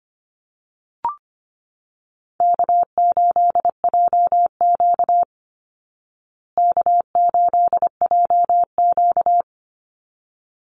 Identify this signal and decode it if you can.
Morse code, heard as a keyed tone, about 25 wpm, 700 Hz. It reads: K8JQ K8JQ
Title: Call Signs Repeated 3x - 25wpm